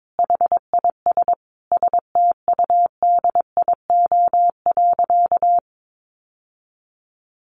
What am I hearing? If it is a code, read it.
Morse code, heard as a keyed tone, about 22 wpm, 700 Hz. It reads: HIS STUDIO.
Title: Sentences from Top 1200 Words 22wpm